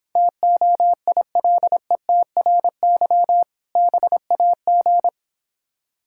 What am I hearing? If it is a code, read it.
Morse code, heard as a keyed tone, about 26 wpm, 700 Hz. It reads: TOILETRY BAG